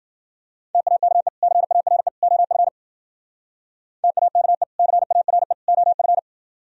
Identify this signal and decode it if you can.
Morse code, heard as a keyed tone, about 50 wpm, 700 Hz. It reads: TAKE CARE OF TAKE CARE OF